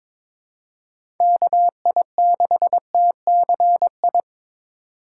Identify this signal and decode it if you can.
Morse code, heard as a keyed tone, about 22 wpm, 700 Hz. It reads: KI6TCI
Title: USA Call Signs - 2x Character Spacing - 22wpm